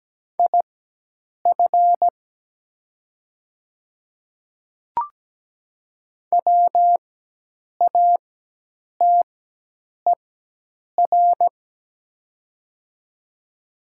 Morse code, heard as a keyed tone, about 17 wpm, spaced as for 8 wpm, 700 Hz. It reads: IF WATER
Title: 4x Character Spacing - Top 100 Words - 17wpm